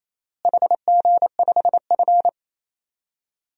Morse code, heard as a keyed tone, about 28 wpm, 700 Hz. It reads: HG5F
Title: DXCC-100 Call Signs - 4 Characters - Repeated 3x - 28wpm